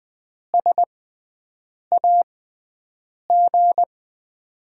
Morse code, heard as a keyed tone, about 20 wpm, spaced as for 7 wpm, 700 Hz. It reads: SAG
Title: USA Call Signs - 6x Character Spacing - 20wpm